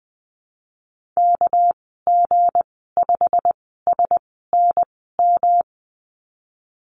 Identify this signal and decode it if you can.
Morse code, heard as a keyed tone, about 20 wpm, 700 Hz. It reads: K G 5 S N M
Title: 200 Lesson 32g - Letter k - Call Signs 20wpm